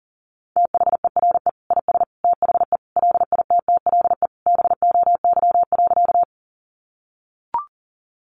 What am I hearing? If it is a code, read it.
Morse code, heard as a keyed tone, about 37 wpm, 700 Hz. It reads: THERE IS THE LITTLE BOY.
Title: Sentences from Top 200 Words 40wpm